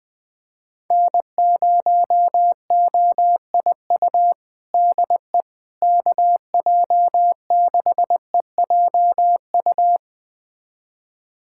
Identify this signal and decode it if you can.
Morse code, heard as a keyed tone, about 20 wpm, 700 Hz. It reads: N0OIU DE KJ6EJU
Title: USA Callsign DE Callsign 20wpm